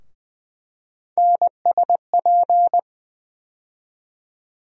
Morse code, heard as a keyed tone, about 20 wpm, 700 Hz. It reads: NSP
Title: USA Call Sign Postfixes - 3 Letters 20wpm